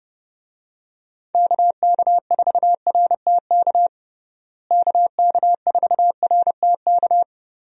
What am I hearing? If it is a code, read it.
Morse code, heard as a keyed tone, about 30 wpm, 700 Hz. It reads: KK4RTK KK4RTK
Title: USA Call Signs - 6 Characters - Repeated 2x - 30wpm